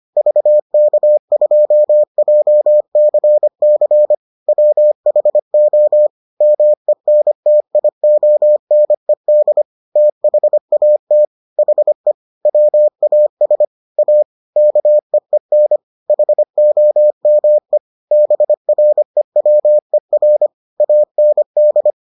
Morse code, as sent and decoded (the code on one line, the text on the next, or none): ...- -.- ..--- .--- -.-. -.-. / .-- .... --- / -- . -. - .. --- -. -.. / - .... .- - / .... / .-- .- ... / .- / -.- . -. / .... --- -- . / -... .-. .-- . .-. / .- -. -..
VK2JCC WHO MENTIONED THAT HE WAS A KEEN HOME BREWER AND